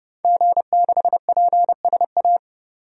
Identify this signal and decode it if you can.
Morse code, heard as a keyed tone, about 30 wpm, 700 Hz. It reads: G6PSA
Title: DXCC-100 Call Signs - 5 Characters - 30wpm